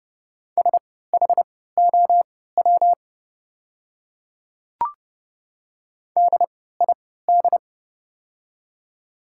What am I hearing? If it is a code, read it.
Morse code, heard as a keyed tone, about 30 wpm, spaced as for 17 wpm, 700 Hz. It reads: SHOW DID
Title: 3x Character Spacing - Top 300 Words - 30wpm